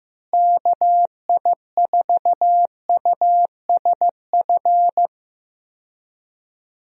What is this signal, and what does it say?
Morse code, heard as a keyed tone, about 15 wpm, 700 Hz. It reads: KI4USF
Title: USA Call Signs - 6 Characters - Repeated 2x - 15wpm